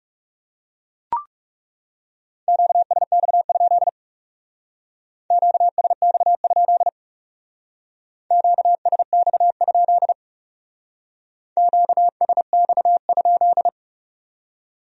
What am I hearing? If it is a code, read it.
Morse code, heard as a keyed tone, about 40 wpm, 700 Hz. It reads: QSX? QSX? QSX? QSX?